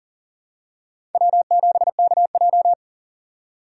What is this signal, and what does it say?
Morse code, heard as a keyed tone, about 40 wpm, 700 Hz. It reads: W7KJ
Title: USA Call Signs - 4 Characters - 4x Character Spacing - 40wpm